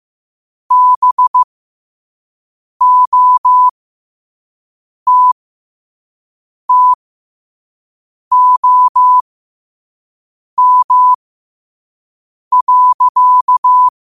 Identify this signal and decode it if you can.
Morse code, heard as a keyed tone, about 15 wpm, spaced as for 5 wpm, 1000 Hz. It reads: BOTTOM.
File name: qotdfiles / 5wpm